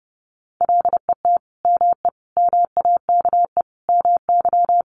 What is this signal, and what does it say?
Morse code, heard as a keyed tone, about 30 wpm, 700 Hz. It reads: LET ME MAKE MY
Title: Sentences from Top 1300 Words 30wpm